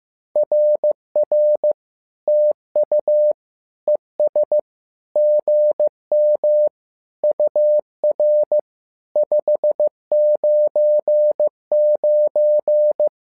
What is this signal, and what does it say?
Morse code, heard as a keyed tone, about 15 wpm, 600 Hz. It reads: RR TU ES GM UR 599
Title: SOTA QSO Practice 15wpm